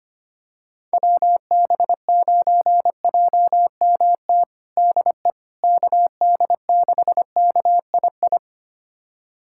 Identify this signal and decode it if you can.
Morse code, heard as a keyed tone, about 25 wpm, 700 Hz. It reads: WB9JMT DE KD6KII
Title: USA Callsign DE Callsign - 6 Characters - 25wpm